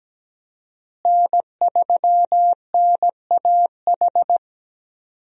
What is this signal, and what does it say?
Morse code, heard as a keyed tone, about 17 wpm, 700 Hz. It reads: N3NAH
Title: USA Call Signs - 5 Characters - Repeated 2x - 17wpm